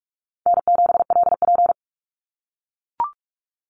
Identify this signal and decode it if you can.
Morse code, heard as a keyed tone, about 44 wpm, 700 Hz. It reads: N7LP
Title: NA SOTA Chaser Callsigns 45wpm